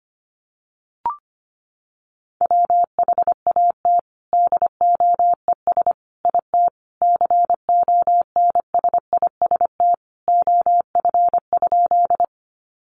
Morse code, heard as a keyed tone, about 25 wpm, 700 Hz. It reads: WHAT DOES IT CONSIST OF?
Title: Sentences from Top 1400 Words 25wpm